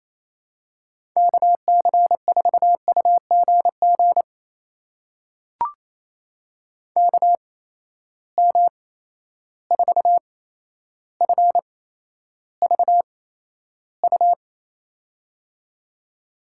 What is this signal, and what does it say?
Morse code, heard as a keyed tone, about 28 wpm, spaced as for 12 wpm, 700 Hz. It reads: KC4UGG KM4FVU
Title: USA Call Signs - 6 Characters - 8x Character Spacing - 28wpm